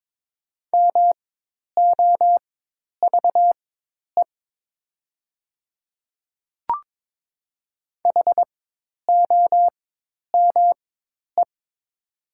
Morse code, heard as a keyed tone, about 22 wpm, spaced as for 10 wpm, 700 Hz. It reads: MOVE HOME
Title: 4x Character Spacing - Top 200 Words - 22wpm